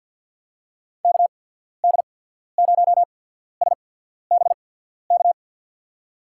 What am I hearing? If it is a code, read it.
Morse code, heard as a keyed tone, about 52 wpm, spaced as for 14 wpm, 700 Hz. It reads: KD0SBK
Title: USA Call Signs - 6 Characters - 8x Character Spacing - 50wpm